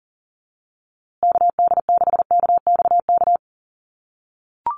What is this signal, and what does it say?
Morse code, heard as a keyed tone, about 40 wpm, 700 Hz. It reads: KD6KXK
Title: USA Call Signs Repeated 2x 40wpm